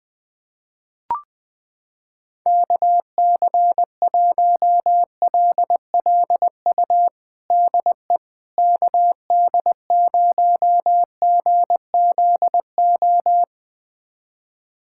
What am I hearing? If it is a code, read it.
Morse code, heard as a keyed tone, about 21 wpm, 700 Hz. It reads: KC1LLU DE KD0GZO
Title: USA Callsign DE Callsign 20wpm